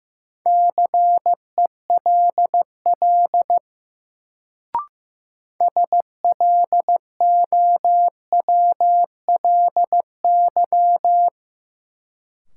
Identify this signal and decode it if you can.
Morse code, heard as a keyed tone, about 15 wpm, 700 Hz. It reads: CELL SLOWLY
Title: Sentences from Top 1100 Words - New Word Drill Alt - 15wpm